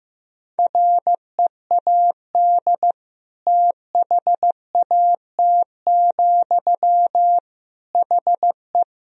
Morse code, heard as a keyed tone, about 15 wpm, 700 Hz. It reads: READ THAT, HE
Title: Sentences from Top 200 Words and Abbreviations 15wpm